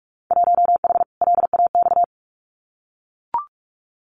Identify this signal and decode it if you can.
Morse code, heard as a keyed tone, about 44 wpm, 700 Hz. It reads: 1H LAX
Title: Winter Field Day Exchanges 45wpm